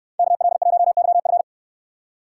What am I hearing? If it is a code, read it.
Morse code, heard as a keyed tone, about 69 wpm, 700 Hz. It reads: KL1YF